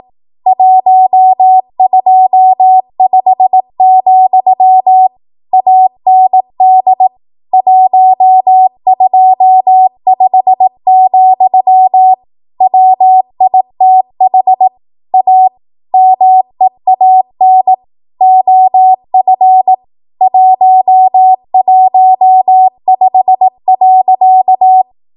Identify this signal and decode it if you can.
Morse code, heard as a keyed tone, about 18 wpm, 750 Hz. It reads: 125, AND 125, WITH A MEAN OF 115.
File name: Morse / 18WPM